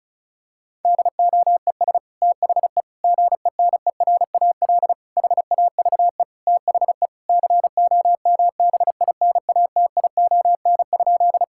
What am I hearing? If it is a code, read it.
Morse code, heard as a keyed tone, about 36 wpm, 700 Hz. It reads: DOES THE GENERAL HAVE THE COMBINATION?
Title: Sentences from Top 1000 Words - Review Part 6 - 35wpm